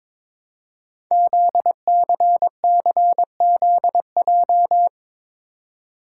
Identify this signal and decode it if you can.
Morse code, heard as a keyed tone, about 22 wpm, 700 Hz. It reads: ZCCZJ
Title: Code Groups - 5 Letters - 22wpm